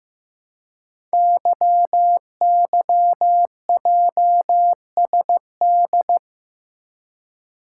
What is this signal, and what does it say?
Morse code, heard as a keyed tone, about 15 wpm, 700 Hz. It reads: YYJSD
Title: Code Groups - 5 Letters - 15wpm